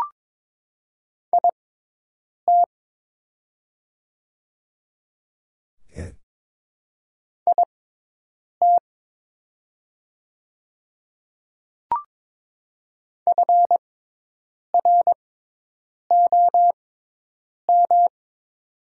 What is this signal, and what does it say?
Morse code, heard as a keyed tone, about 22 wpm, spaced as for 8 wpm, 700 Hz. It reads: IT IT FROM
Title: 6x Character Spacing - Top 100 Words - 22wpm